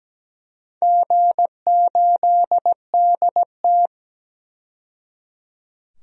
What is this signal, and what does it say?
Morse code, heard as a keyed tone, about 17 wpm, 700 Hz. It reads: G8DT
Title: DXCC-100 Call Signs - 4 Characters - 17wpm